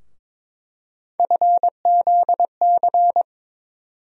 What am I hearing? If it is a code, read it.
Morse code, heard as a keyed tone, about 22 wpm, 700 Hz. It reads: FZC
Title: USA Call Sign Postfixes - 3 Letters 22wpm